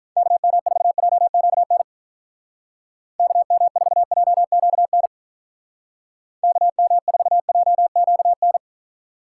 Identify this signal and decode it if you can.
Morse code, heard as a keyed tone, about 43 wpm, 700 Hz. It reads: KM4JQN KM4JQN KM4JQN
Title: Speed Racing - USA 2x3 Call Signs 35wpm